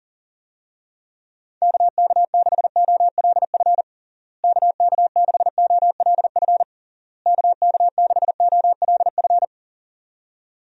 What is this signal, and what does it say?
Morse code, heard as a keyed tone, about 40 wpm, 700 Hz. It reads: KK6OLF KK6OLF KK6OLF
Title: USA Call Signs - 6 Characters - Repeated 3x - 40wpm